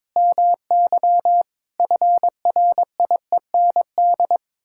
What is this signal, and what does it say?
Morse code, heard as a keyed tone, about 22 wpm, 700 Hz. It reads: MY FRIEND
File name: practice / 22wpm / Sets of 8 Words - 22wpm